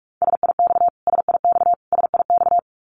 Morse code, heard as a keyed tone, about 44 wpm, 700 Hz. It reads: SIX SIX SIX
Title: Top 500 Words Repeated 45wpm